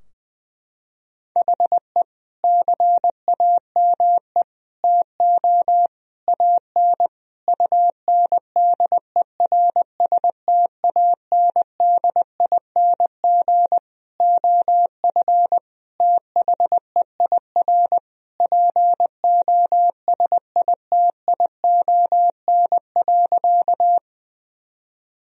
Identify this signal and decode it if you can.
Morse code, heard as a keyed tone, about 21 wpm, 700 Hz. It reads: HE CAME TO AN UNDERSTANDING OF THEIR POSITION.
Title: Sentences from Top 1000 Words - Review Part 3 - 20wpm